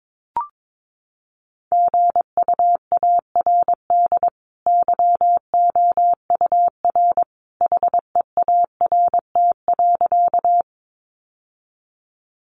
Morse code, heard as a keyed tone, about 22 wpm, 700 Hz. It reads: GUARD YOUR HEART.
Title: Sentences from Top 1200 Words 22wpm